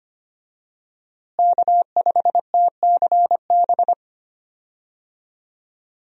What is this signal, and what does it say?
Morse code, heard as a keyed tone, about 25 wpm, 700 Hz. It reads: K5TCB
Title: USA Call Signs - 5 Characters - Repeated 2x - 25wpm